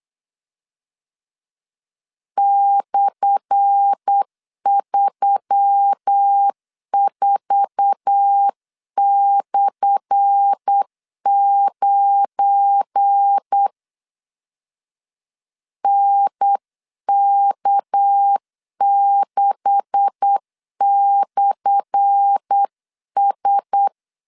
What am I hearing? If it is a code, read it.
Morse code, heard as a keyed tone, about 8 wpm, 800 Hz. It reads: /34/9 NK6/S
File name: LEZIONI / 10 SIGNS / 10A SLASH 8wpm 3452